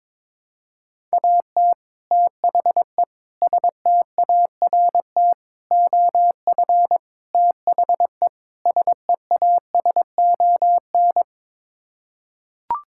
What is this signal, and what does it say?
Morse code, heard as a keyed tone, about 22 wpm, 700 Hz. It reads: AT THE START OF THE SEASON
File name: practice / 22wpm / Sets of 6 Words - 22wpm